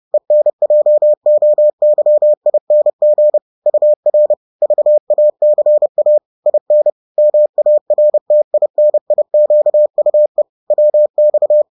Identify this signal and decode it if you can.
Morse code, heard as a keyed tone, about 30 wpm, 600 Hz. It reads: ENJOYING UR VACA IN MARTINIQUE WX